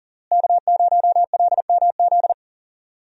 Morse code, heard as a keyed tone, about 40 wpm, 700 Hz. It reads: K0LMZ
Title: USA Call Signs - 5 Characters - Repeated 3x - 40wpm